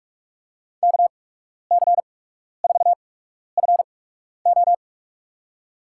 Morse code, heard as a keyed tone, about 46 wpm, spaced as for 12 wpm, 700 Hz. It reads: KC4FO
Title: USA Call Signs - 5 Characters - 8x Character Spacing - 45wpm